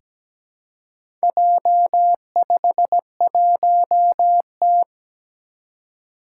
Morse code, heard as a keyed tone, about 17 wpm, 700 Hz. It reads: J51T
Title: Four Challenging Letter-Number Combinations 17wpm